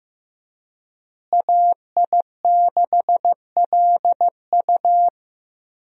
Morse code, heard as a keyed tone, about 15 wpm, 700 Hz. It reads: AI6LU
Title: USA Call Signs - 5 Characters - Repeated 3x - 15wpm